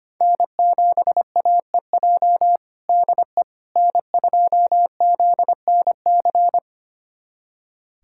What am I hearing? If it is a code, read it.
Morse code, heard as a keyed tone, about 25 wpm, 700 Hz. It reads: N7AEJ DE N2ZNC
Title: USA Callsign DE Callsign 25wpm